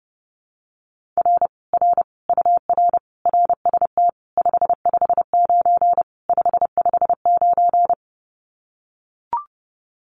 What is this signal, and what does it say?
Morse code, heard as a keyed tone, about 30 wpm, 700 Hz. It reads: R R UR RST 559 559
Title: Common QSO Phrases Encore 30wpm